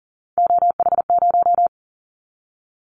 Morse code, heard as a keyed tone, about 38 wpm, 700 Hz. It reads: OH0
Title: Callsign Prefixes for Top 100 DXCC - 40wpm